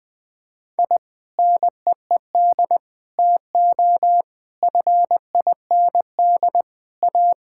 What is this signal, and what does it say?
Morse code, heard as a keyed tone, about 20 wpm, 700 Hz. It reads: I NEED TO FIND A
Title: Sentences with 3-Word Phrases - Spoken Beginning - 20wpm